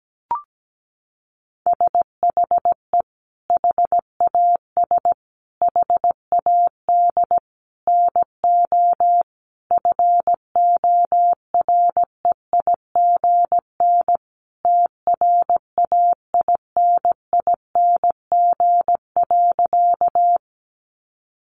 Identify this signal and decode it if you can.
Morse code, heard as a keyed tone, about 17 wpm, 700 Hz. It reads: SHE HAS HAD NO FOREIGN TRAINING.
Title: Sentences from Top 1400 Words 17wpm